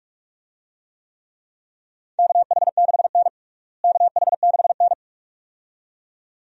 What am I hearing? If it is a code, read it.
Morse code, heard as a keyed tone, about 46 wpm, 700 Hz. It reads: KH6N KH6N
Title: DXCC-100 Call Signs - 4 Characters - Repeated 2x - 45wpm